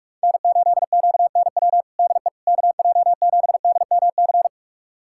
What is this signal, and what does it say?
Morse code, heard as a keyed tone, about 46 wpm, 700 Hz. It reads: N8QNW DE KJ7DMC
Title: USA Callsign DE Callsign 45wpm